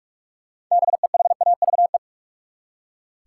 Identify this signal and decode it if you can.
Morse code, heard as a keyed tone, about 44 wpm, 700 Hz. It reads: BEHAVE
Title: Six Letter Words - 45wpm